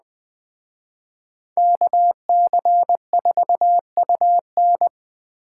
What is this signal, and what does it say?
Morse code, heard as a keyed tone, about 20 wpm, 700 Hz. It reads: KC4UN
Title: Speed Racing - USA 2x2 Call Signs 17wpm